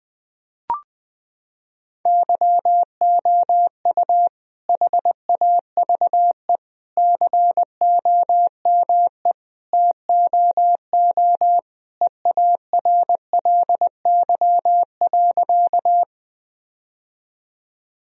Morse code, heard as a keyed tone, about 20 wpm, 700 Hz. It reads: YOU HAVE COME TOO EARLY.
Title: Sentences from Top 1000 Words - Review Part 2 - 20wpm